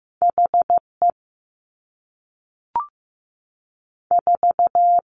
Morse code, heard as a keyed tone, about 15 wpm, 700 Hz. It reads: HE 4
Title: Two Challenging Letter-Number Combinations 15wpm